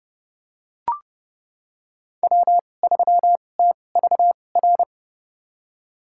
Morse code, heard as a keyed tone, about 31 wpm, spaced as for 22 wpm, 700 Hz. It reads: W3TVR
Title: USA Call Signs - 5 Characters - 2x Character Spacing - 30wpm